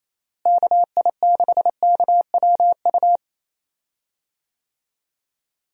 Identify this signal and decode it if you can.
Morse code, heard as a keyed tone, about 28 wpm, 700 Hz. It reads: KI6KWU
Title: USA Call Signs - 6 Characters - Repeated 2x - 28wpm